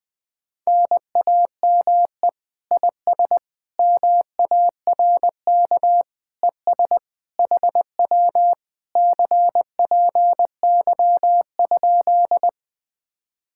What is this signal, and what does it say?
Morse code, heard as a keyed tone, about 20 wpm, 700 Hz. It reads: NAME IS MARK ES HW CPY?
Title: Speed-Racing Common QSO Phrases - 20wpm